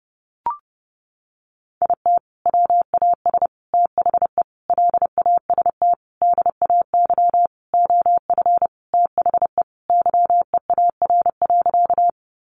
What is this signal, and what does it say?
Morse code, heard as a keyed tone, about 30 wpm, 700 Hz. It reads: IT WAS THE LAST DAY OF THE YEAR.
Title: Sentences from Top 300 Words 30wpm